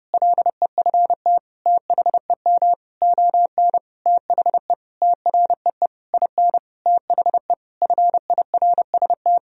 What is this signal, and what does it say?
Morse code, heard as a keyed tone, about 32 wpm, 700 Hz. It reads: LEFT THEM ON THE TREE IN THE FIRST
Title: Sentences from Top 1000 Words - Review Part 1 - 30wpm